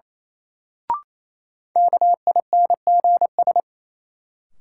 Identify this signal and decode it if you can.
Morse code, heard as a keyed tone, about 27 wpm, 700 Hz. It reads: KINGS
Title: Sentences from Top 1000 Words - Review Part 6 - New Word Drill - 28wpm